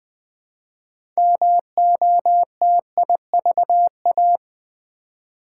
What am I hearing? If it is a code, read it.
Morse code, heard as a keyed tone, about 20 wpm, 700 Hz. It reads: MOTIVA
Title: Build A Word - 3 Letters At A Time - 8-Letter Words - 20wpm